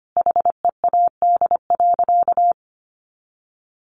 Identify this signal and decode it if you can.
Morse code, heard as a keyed tone, about 25 wpm, 700 Hz. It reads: HEAD.
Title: Sentences from Top 1100 Words 25wpm